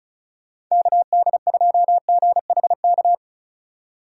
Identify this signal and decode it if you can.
Morse code, heard as a keyed tone, about 36 wpm, 700 Hz. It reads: KD2GHK
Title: USA Call Signs - 4x Character Spacing - 35wpm